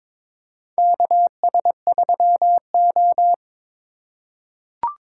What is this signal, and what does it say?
Morse code, heard as a keyed tone, about 22 wpm, 700 Hz. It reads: KS3O
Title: USA Call Signs - 4 Characters - 2x Character Spacing - 22wpm